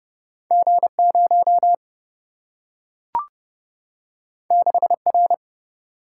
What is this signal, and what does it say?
Morse code, heard as a keyed tone, about 30 wpm, 700 Hz. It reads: G0 6R
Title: Single-Letter Single-Number 30wpm